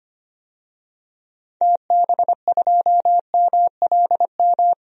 Morse code, heard as a keyed tone, about 25 wpm, 700 Hz. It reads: TB2MLM
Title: DXCC-100 Call Signs - 6 Characters - 25wpm